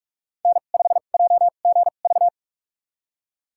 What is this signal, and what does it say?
Morse code, heard as a keyed tone, about 46 wpm, spaced as for 32 wpm, 700 Hz. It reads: N5JGV
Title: USA Call Signs - 2x Character Spacing - 45wpm